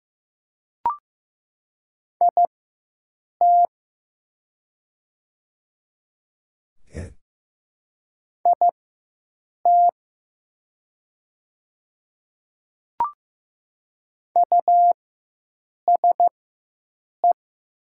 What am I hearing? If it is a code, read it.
Morse code, heard as a keyed tone, about 15 wpm, spaced as for 7 wpm, 700 Hz. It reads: IT IT USE
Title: 4x Character Spacing - Top 100 Words - 15wpm